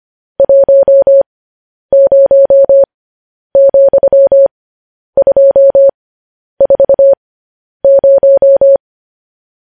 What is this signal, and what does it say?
Morse code, heard as a keyed tone, about 25 wpm, spaced as for 10 wpm, 550 Hz. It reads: 10,240
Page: news headlines in morse code at 10 wpm